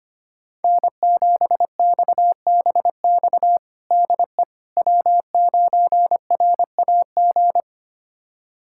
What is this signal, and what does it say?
Morse code, heard as a keyed tone, about 25 wpm, 700 Hz. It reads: N7XBX DE W9RAG
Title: USA Callsign DE Callsign - 5 Characters - 25wpm